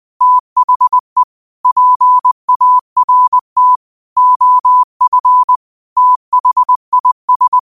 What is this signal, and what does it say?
Morse code, heard as a keyed tone, about 20 wpm, 1000 Hz. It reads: THE PART OF THIS